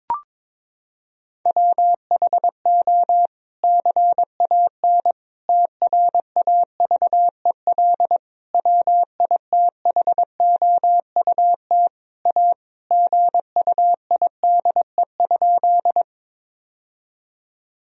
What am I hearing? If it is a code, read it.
Morse code, heard as a keyed tone, about 22 wpm, 700 Hz. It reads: WHO CAN TRAVEL WITHOUT A GUIDE?
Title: Sentences from Top 1000 Words - Review Part 3 - 22wpm